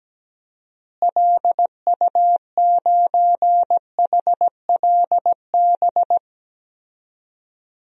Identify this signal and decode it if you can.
Morse code, heard as a keyed tone, about 17 wpm, 700 Hz. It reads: LU9HLB
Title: DXCC-100 Call Signs - 6 Characters - Repeated 2x - 17wpm